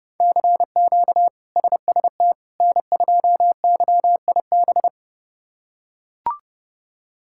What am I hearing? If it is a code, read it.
Morse code, heard as a keyed tone, about 30 wpm, 700 Hz. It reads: CQ SST N2YIB
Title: K1USN SST QSO Practice - 30wpm